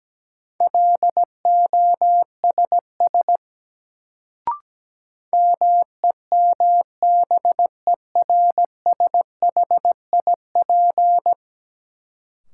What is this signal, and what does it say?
Morse code, heard as a keyed tone, about 17 wpm, 700 Hz. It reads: LOSS MEMBERSHIP
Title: Sentences from Top 1400 Words - New Word Drill Alt - 17wpm